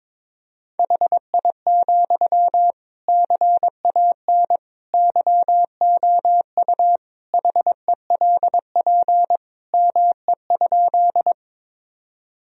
Morse code, heard as a keyed tone, about 22 wpm, 700 Hz. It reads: HI, CAN YOU HELP ME?
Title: Sentences from Top 1000 Words - Review Part 3 - 22wpm